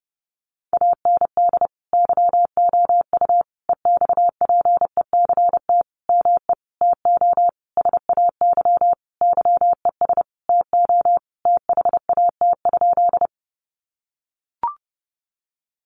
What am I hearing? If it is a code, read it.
Morse code, heard as a keyed tone, about 28 wpm, 700 Hz. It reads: AND YOU EXPECT ME TO SAY YES TO THAT?
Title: Sentences from Top 1100 Words 30wpm